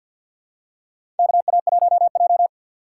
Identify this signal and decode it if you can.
Morse code, heard as a keyed tone, about 48 wpm, 700 Hz. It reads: KA1J
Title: USA Call Signs - 4x Character Spacing - 50wpm